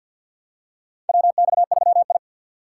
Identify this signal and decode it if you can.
Morse code, heard as a keyed tone, about 47 wpm, 700 Hz. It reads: WX3I
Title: USA Call Signs - 4 Characters - 6x Character Spacing - 50wpm